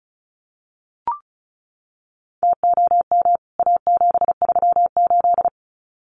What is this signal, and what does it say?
Morse code, heard as a keyed tone, about 34 wpm, 700 Hz. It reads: TOM A738
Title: CWT Exchange 35wpm